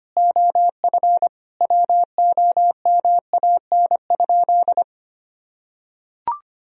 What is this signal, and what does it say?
Morse code, heard as a keyed tone, about 25 wpm, 700 Hz. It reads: OF WOMAN?
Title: Sentences from Top 1000 Words - Review Part 3 - 25wpm